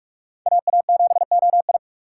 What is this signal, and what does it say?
Morse code, heard as a keyed tone, about 51 wpm, 700 Hz. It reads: AA7OI
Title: Speed Racing - USA 2x2 Call Signs 45wpm